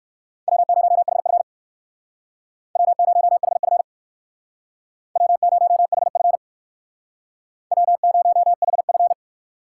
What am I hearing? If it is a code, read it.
Morse code, heard as a keyed tone, about 58 wpm, 700 Hz. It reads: W0HF W0HF W0HF W0HF